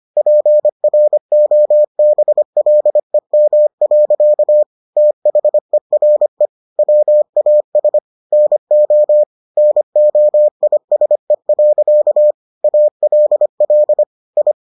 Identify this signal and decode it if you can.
Morse code, heard as a keyed tone, about 25 wpm, 600 Hz. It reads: PROBLEM. THERE WAS NO NOISE. ALL I